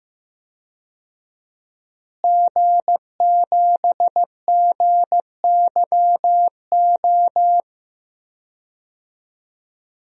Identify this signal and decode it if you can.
Morse code, heard as a keyed tone, about 15 wpm, 700 Hz. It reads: G7GYO
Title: DXCC-100 Call Signs - 5 Characters - Repeated 2x - 15wpm